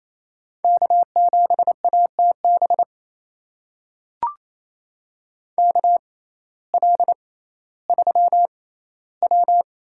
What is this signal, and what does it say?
Morse code, heard as a keyed tone, about 28 wpm, spaced as for 15 wpm, 700 Hz. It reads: K7ATB KL3W
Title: USA Call Signs - 5 Characters - 6x Character Spacing - 28wpm